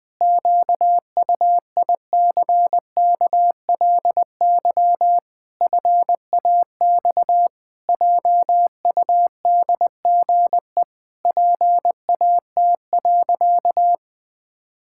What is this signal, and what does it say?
Morse code, heard as a keyed tone, about 19 wpm, 700 Hz. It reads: QUICKLY FAX JUDGE PAT.
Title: Pangram Challenge v2 - 20wpm